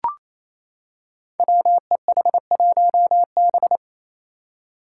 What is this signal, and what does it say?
Morse code, heard as a keyed tone, about 29 wpm, 700 Hz. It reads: WEH1B
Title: Five Challenging Letter-Number Combinations 28wpm